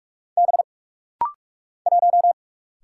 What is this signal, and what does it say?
Morse code, heard as a keyed tone, about 43 wpm, 700 Hz. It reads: B 1
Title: Single Letter-Number - Rapid-Fire 45wpm